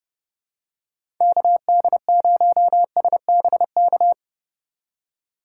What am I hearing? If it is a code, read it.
Morse code, heard as a keyed tone, about 31 wpm, 700 Hz. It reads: KD0SBK
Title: USA Call Signs - 6 Characters - 3x Character Spacing - 30wpm